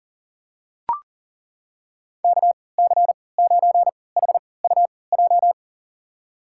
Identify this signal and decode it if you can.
Morse code, heard as a keyed tone, about 40 wpm, spaced as for 23 wpm, 700 Hz. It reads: KC9HUJ
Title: USA Call Signs - 6 Characters - 3x Character Spacing - 40wpm